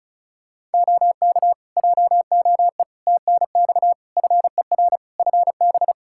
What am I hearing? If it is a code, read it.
Morse code, heard as a keyed tone, about 34 wpm, 700 Hz. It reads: OK JOE TNX FER FB
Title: K1USN SST QSO Practice 35wpm